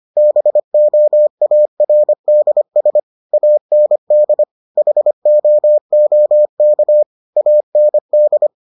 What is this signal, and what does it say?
Morse code, heard as a keyed tone, about 25 wpm, 600 Hz. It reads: BOARDS AND HOOK AND